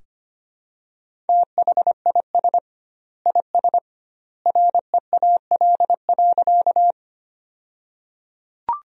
Morse code, heard as a keyed tone, about 25 wpm, 700 Hz. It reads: THIS IS REAL.
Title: 2x Word-Spacing - Sentences from Top 300 Words - 25wpm